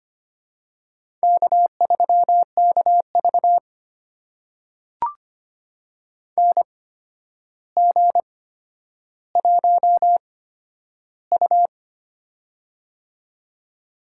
Morse code, heard as a keyed tone, about 25 wpm, spaced as for 11 wpm, 700 Hz. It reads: K3KV NG1U
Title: USA Call Signs - 4 Characters - 8x Character Spacing - 25wpm